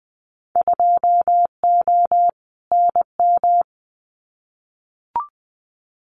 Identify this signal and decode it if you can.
Morse code, heard as a keyed tone, about 20 wpm, 700 Hz. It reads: 2O NM
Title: Winter Field Day Exchanges 20wpm